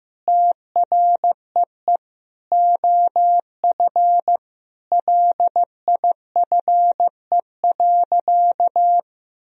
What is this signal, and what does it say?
Morse code, heard as a keyed tone, about 15 wpm, 700 Hz. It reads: TREE OF LIFE.
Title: Sentences from Top 1000 Words - Review Part 1 - 15wpm